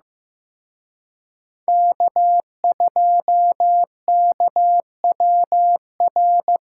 Morse code, heard as a keyed tone, about 18 wpm, 700 Hz. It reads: K2KWR
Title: Speed Racing - USA 1x3 Call Signs 15wpm